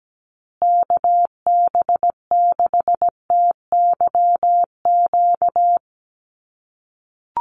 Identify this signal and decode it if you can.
Morse code, heard as a keyed tone, about 17 wpm, 700 Hz. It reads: KB6TYQ
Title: USA Call Signs Repeated 2x 17wpm